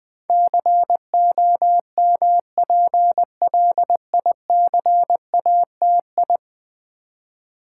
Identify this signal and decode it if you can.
Morse code, heard as a keyed tone, about 20 wpm, 700 Hz. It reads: COMPLICATI
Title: Build A Word - 5 Letters At A Time - 12-Letter Words - 20wpm